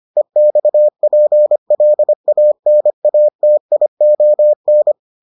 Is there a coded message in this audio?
Yes: Morse code from 0.2 to 4.9 s, heard as a keyed tone, about 25 wpm, 600 Hz. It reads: EXPLANATION